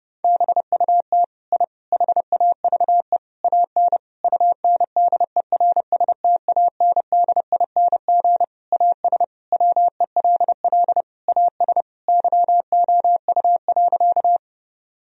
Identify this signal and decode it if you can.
Morse code, heard as a keyed tone, about 32 wpm, 700 Hz. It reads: BUT I HAVE AN UNDERSTANDING AS WELL AS YOU.
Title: Sentences from Top 1000 Words - Review Part 3 - 30wpm